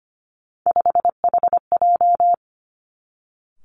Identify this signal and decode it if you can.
Morse code, heard as a keyed tone, about 25 wpm, 700 Hz. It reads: 5HJ
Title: Three Challenging Letter-Number Combinations 25wpm